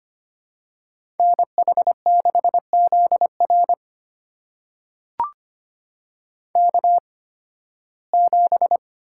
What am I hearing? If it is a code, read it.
Morse code, heard as a keyed tone, about 25 wpm, spaced as for 16 wpm, 700 Hz. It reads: NH6ZR K7
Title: USA Call Signs - 8x Character Spacing - 25wpm